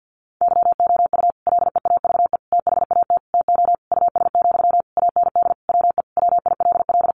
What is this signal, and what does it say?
Morse code, heard as a keyed tone, about 50 wpm, 700 Hz. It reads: YOU LEAVE THAT TO US, AND WE WILL
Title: Sentences from Top 1400 Words 50wpm